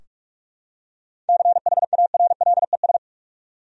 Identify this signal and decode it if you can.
Morse code, heard as a keyed tone, about 46 wpm, 700 Hz. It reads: CHARLES
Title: USA Common Names - 45wpm